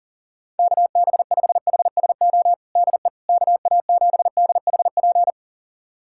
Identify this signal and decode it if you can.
Morse code, heard as a keyed tone, about 40 wpm, 700 Hz. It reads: KB5HSO DE KA7DHP
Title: USA Callsign DE Callsign 40wpm